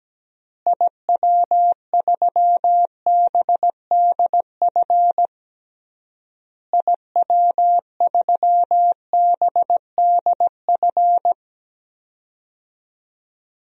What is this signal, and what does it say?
Morse code, heard as a keyed tone, about 17 wpm, 700 Hz. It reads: IW3BDF IW3BDF
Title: DXCC-100 Call Signs - 6 Characters - Repeated 3x - 17wpm